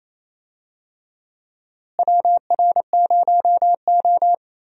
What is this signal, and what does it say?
Morse code, heard as a keyed tone, about 28 wpm, 700 Hz. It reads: WR0O
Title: USA Call Signs - 4 Characters - Repeated 3x - 28wpm